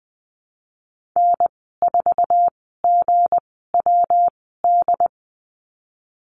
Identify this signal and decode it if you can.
Morse code, heard as a keyed tone, about 20 wpm, spaced as for 15 wpm, 700 Hz. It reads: N4GWD
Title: 193 Lesson 31g - Letter v - Call Signs 20wpm